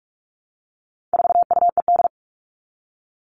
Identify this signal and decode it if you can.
Morse code, heard as a keyed tone, about 44 wpm, 700 Hz. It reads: 4UED
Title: All Cut Numbers - 4-Digit 45wpm